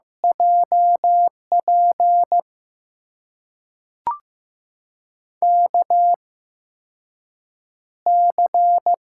Morse code, heard as a keyed tone, about 15 wpm, spaced as for 6 wpm, 700 Hz. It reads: JP KC
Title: USA Call Signs - 6 Characters - 8x Character Spacing - 15wpm